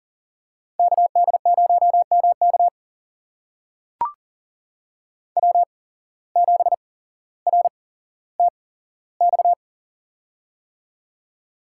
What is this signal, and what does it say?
Morse code, heard as a keyed tone, about 38 wpm, spaced as for 17 wpm, 700 Hz. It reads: KD0MK W7RTX
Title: USA Call Signs - 5 Characters - 8x Character Spacing - 40wpm